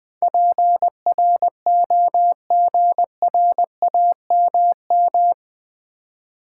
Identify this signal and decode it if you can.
Morse code, heard as a keyed tone, about 20 wpm, 700 Hz. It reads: PROGRAMM
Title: Build A Word - 4 Letters At A Time - 11-Letter Words - 20wpm